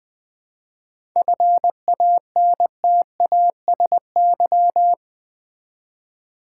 Morse code, heard as a keyed tone, about 20 wpm, 700 Hz. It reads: FANTASY
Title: 7 Letter Words - First 3 Letters Given - 20wpm